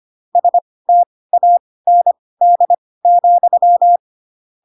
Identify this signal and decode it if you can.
Morse code, heard as a keyed tone, about 25 wpm, spaced as for 18 wpm, 700 Hz. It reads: STAND,
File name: Pinocchio / Pinocchio ch6 12 25 ik1hge